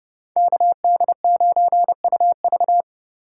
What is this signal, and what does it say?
Morse code, heard as a keyed tone, about 31 wpm, 700 Hz. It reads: KD9UV